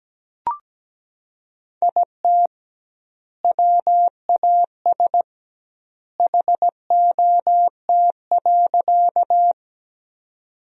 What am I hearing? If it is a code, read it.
Morse code, heard as a keyed tone, about 17 wpm, 700 Hz. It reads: IT WAS HOT.
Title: 2x Word-Spacing - Sentences from Top 500 Words - 17wpm